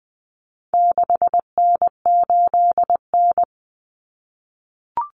Morse code, heard as a keyed tone, about 20 wpm, 700 Hz. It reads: 6N8N
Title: Common Cut Numbers - 4-Digit 20wpm